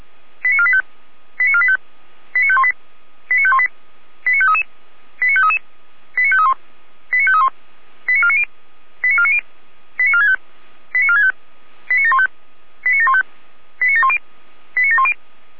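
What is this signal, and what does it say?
Five-tone selective call (ZVEI), heard as quick runs of five tones, tones of about 70 ms. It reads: 89485 89485 89518 89518 89533 89533 89531 89531 89490 89490 89465 89465 89715 89715 89710 89710
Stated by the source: ZVEI five-tone call received on an SDR.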